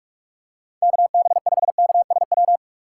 Morse code, heard as a keyed tone, about 43 wpm, 700 Hz. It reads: KB5KSW
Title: USA Call Signs - 4x Character Spacing - 45wpm